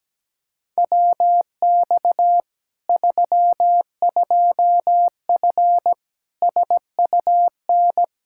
Morse code, heard as a keyed tone, about 17 wpm, 700 Hz. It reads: WX 32F SUN
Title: Practice QSO - Exchange 2 - Info - 17wpm